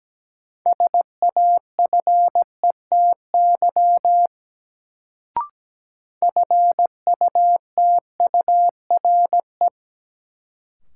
Morse code, heard as a keyed tone, about 17 wpm, 700 Hz. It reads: SAFETY FUTURE
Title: Sentences from Top 700 Words - New Word Drill - 17wpm